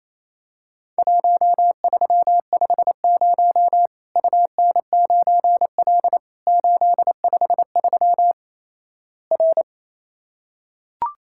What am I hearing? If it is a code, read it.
Morse code, heard as a keyed tone, about 27 wpm, 700 Hz. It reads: 1350 UN9L 853 R
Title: Worked All Europe DX Contest - QTC Exchange - 28wpm